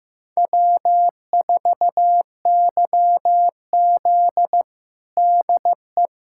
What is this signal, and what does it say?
Morse code, heard as a keyed tone, about 15 wpm, 700 Hz. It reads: W4YZ DE
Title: USA Callsign DE Callsign - 4 Characters - 15wpm